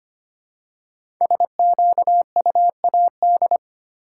Morse code, heard as a keyed tone, about 25 wpm, 700 Hz. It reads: SQUAD
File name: 5 Letter Words - First 3 Letters Given - 25wpm